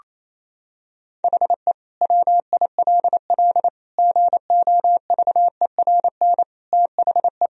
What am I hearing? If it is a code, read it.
Morse code, heard as a keyed tone, about 28 wpm, 700 Hz. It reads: HE WILL GOVERN THE
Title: Sentences from Top 1000 Words - Review Part 2 - 28wpm